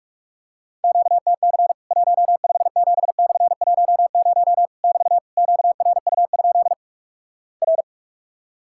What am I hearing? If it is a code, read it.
Morse code, heard as a keyed tone, about 46 wpm, 700 Hz. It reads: QTC 157/10 <BT> QRU? R
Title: Worked All Europe DX Contest - QTC Exchange - 45wpm